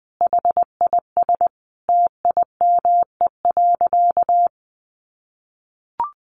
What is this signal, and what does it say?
Morse code, heard as a keyed tone, about 20 wpm, 700 Hz. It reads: HIS TIME.
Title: Sentences from Top 1400 Words 20wpm